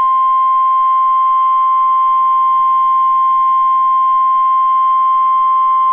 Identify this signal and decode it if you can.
Emergency Alert System (SAME) tones (the 1050 Hz attention tone). It is heard from 0.3 to 5.3 s.